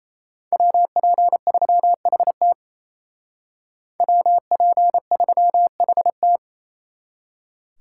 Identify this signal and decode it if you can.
Morse code, heard as a keyed tone, about 33 wpm, 700 Hz. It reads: WP3HT WP3HT
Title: Speed Racing - USA 2x2 Call Signs 28wpm